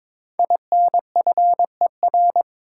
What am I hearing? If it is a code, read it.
Morse code, heard as a keyed tone, about 22 wpm, 700 Hz. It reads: INFER
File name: 5 Letter Words - First 3 Letters Given - 22wpm